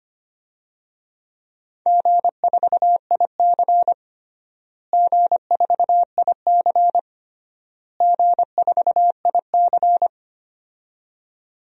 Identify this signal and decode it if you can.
Morse code, heard as a keyed tone, about 25 wpm, 700 Hz. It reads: G4IC G4IC G4IC
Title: DXCC-100 Call Signs - 4 Characters - Repeated 3x - 25wpm